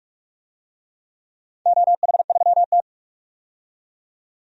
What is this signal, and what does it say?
Morse code, heard as a keyed tone, about 44 wpm, 700 Hz. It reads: OH3T
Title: DXCC-100 Call Signs - 4 Characters - 45wpm